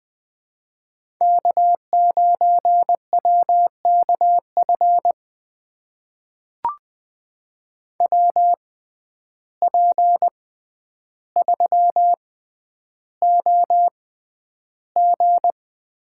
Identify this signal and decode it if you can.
Morse code, heard as a keyed tone, about 20 wpm, spaced as for 10 wpm, 700 Hz. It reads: K9WKF WP3OG
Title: USA Call Signs - 5 Characters - 6x Character Spacing - 20wpm